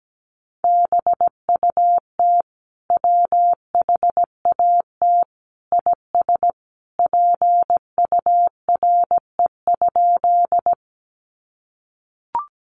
Morse code, heard as a keyed tone, about 17 wpm, 700 Hz. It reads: BUT WHAT IS PURE?
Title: Sentences from Top 1500 Words 17wpm